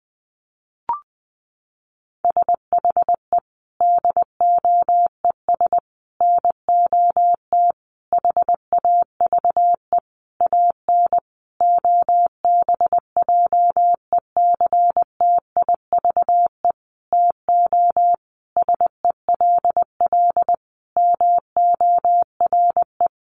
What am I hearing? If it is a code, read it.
Morse code, heard as a keyed tone, about 20 wpm, 700 Hz. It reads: SHE DOES NOT HAVE AN OBJECTIVE TO SELL MORE
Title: Sentences from Top 1200 Words 20wpm